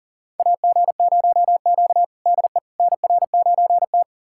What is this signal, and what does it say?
Morse code, heard as a keyed tone, about 39 wpm, 700 Hz. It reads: AG0Q DE NR9T
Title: USA Callsign DE Callsign - 4 Characters - 40wpm